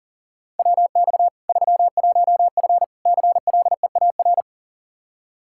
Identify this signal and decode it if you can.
Morse code, heard as a keyed tone, about 40 wpm, 700 Hz. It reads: WX 31F CLEAR
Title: Practice QSO - Exchange 2 - Info - 40wpm